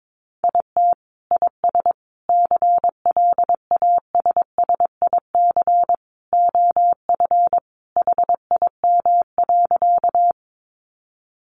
Morse code, heard as a keyed tone, about 22 wpm, 700 Hz. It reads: IT IS CLASSIC OF HIM.
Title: Sentences from Top 1000 Words 22wpm